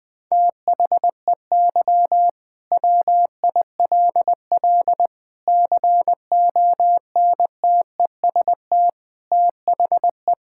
Morse code, heard as a keyed tone, about 20 wpm, 700 Hz. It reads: THEY WILL CONTEST THE
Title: Sentences from Top 1600 Words 20wpm